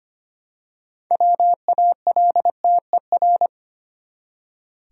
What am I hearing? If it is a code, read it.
Morse code, heard as a keyed tone, about 25 wpm, 700 Hz. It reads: WALTER
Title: USA Common Names - 25wpm